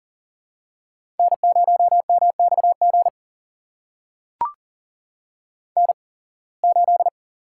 Morse code, heard as a keyed tone, about 40 wpm, spaced as for 26 wpm, 700 Hz. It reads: N0MXG N8
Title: USA Call Signs - 5 Characters - 8x Character Spacing - 40wpm